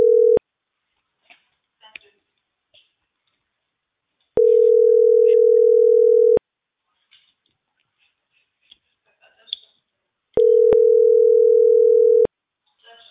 Ringback tone (North America), 440 + 480 Hz.